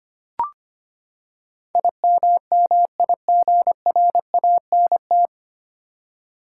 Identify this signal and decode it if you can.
Morse code, heard as a keyed tone, about 25 wpm, 700 Hz. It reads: IMMIGRANT
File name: Nine Letter Words - 25wpm